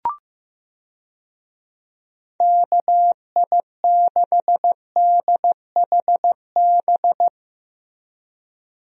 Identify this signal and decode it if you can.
Morse code, heard as a keyed tone, about 15 wpm, 700 Hz. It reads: KI6DHB
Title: USA Call Signs - 6 Characters - Repeated 3x - 15wpm